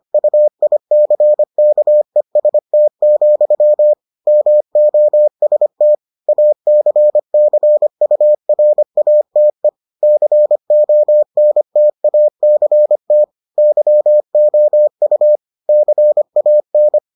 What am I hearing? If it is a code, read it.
Morse code, heard as a keyed tone, about 25 wpm, 600 Hz. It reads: UICKEST, MOST ACCURATE CONTACT YOU CAN